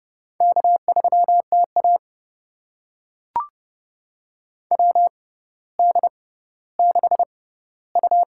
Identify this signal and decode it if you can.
Morse code, heard as a keyed tone, about 30 wpm, spaced as for 15 wpm, 700 Hz. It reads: K3TA WD6U
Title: USA Call Signs - 4 Characters - 6x Character Spacing - 30wpm